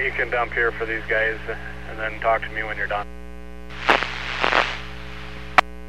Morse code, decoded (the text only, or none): none